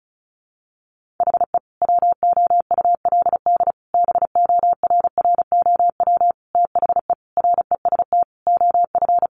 Morse code, heard as a keyed tone, about 34 wpm, 700 Hz. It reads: HE WOULD BORROW THE REST OF
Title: Sentences from Top 1500 Words 35wpm